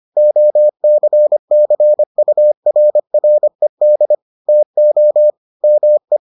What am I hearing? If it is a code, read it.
Morse code, heard as a keyed tone, about 25 wpm, 600 Hz. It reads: OCCURRED TO ME